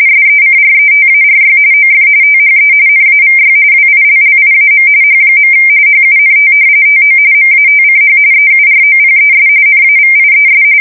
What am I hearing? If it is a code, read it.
RTTY, about 45.45 baud. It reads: SECRETARY OF THE NAVY WILLIAM H. HUNT TO MATCH EUROPES NAVIES
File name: fsk170 / battleships